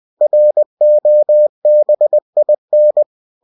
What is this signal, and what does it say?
Morse code, heard as a keyed tone, about 20 wpm, 600 Hz. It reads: ROBIN
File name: cw-books / winnie-the-pooh / winnie-20wpm-page-0105